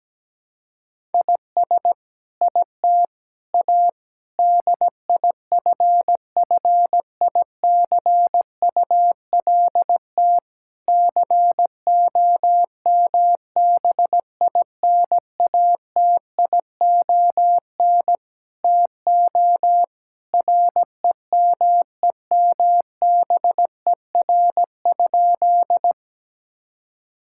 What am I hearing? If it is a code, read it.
Morse code, heard as a keyed tone, about 17 wpm, 700 Hz. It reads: IS IT A DIFFICULT COMBINATION TO REMEMBER?
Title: Sentences from Top 1000 Words - Review Part 6 - 17wpm